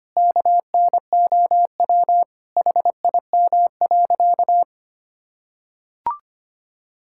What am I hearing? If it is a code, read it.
Morse code, heard as a keyed tone, about 25 wpm, 700 Hz. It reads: KNOW HIM.
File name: practice / 25wpm / Sentences with 4-Word Phrases - 25wpm